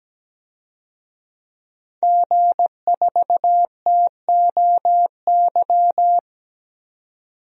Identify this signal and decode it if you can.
Morse code, heard as a keyed tone, about 17 wpm, 700 Hz. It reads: G4TOY
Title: DXCC-100 Call Signs - 5 Characters - 17wpm